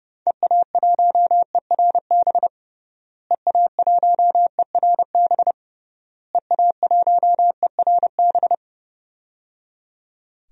Morse code, heard as a keyed tone, about 30 wpm, 700 Hz. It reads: EA1ERB EA1ERB EA1ERB
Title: DXCC-100 Call Signs - 6 Characters - Repeated 3x - 30wpm